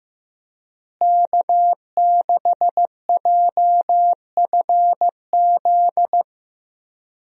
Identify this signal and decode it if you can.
Morse code, heard as a keyed tone, about 15 wpm, 700 Hz. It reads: K6JFZ
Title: USA Call Signs - 5 Characters - Repeated 2x - 15wpm